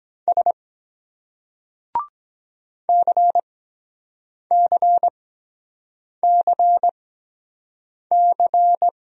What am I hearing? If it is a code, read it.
Morse code, heard as a keyed tone, about 23 wpm, spaced as for 7 wpm, 700 Hz. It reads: S CCCC